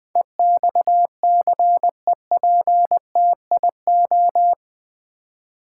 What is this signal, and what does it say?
Morse code, heard as a keyed tone, about 20 wpm, 700 Hz. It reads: EXCEPTIO
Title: Build A Word - 4 Letters At A Time - 11-Letter Words - 20wpm